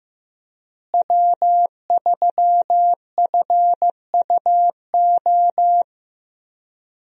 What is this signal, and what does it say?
Morse code, heard as a keyed tone, about 15 wpm, 700 Hz. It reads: W3FUO
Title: USA Call Signs - 5 Characters - Repeated 3x - 15wpm